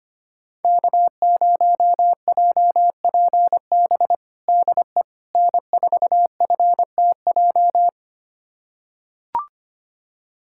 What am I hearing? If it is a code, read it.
Morse code, heard as a keyed tone, about 25 wpm, 700 Hz. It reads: K0JPB DE N4FTJ
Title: USA Callsign DE Callsign - 5 Characters - 25wpm